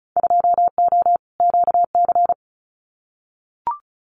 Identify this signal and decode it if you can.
Morse code, heard as a keyed tone, about 34 wpm, 700 Hz. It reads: 2O QC
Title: Winter Field Day Exchanges 35wpm